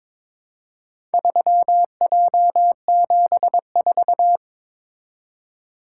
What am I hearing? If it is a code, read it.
Morse code, heard as a keyed tone, about 21 wpm, 700 Hz. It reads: 3J74
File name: practice / 22wpm / Four Letter-Number Combinations - 22wpm